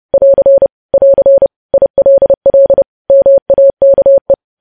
Morse code, heard as a keyed tone, about 30 wpm, 550 Hz. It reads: <AR> <AR> ILL MAKE
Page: news headlines in morse code at 30 wpm